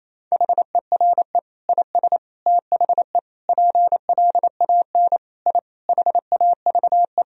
Morse code, heard as a keyed tone, about 27 wpm, 700 Hz. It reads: HERE IS THE PLAN I HAVE
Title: Sentences from Top 1000 Words - Review Part 2 - 28wpm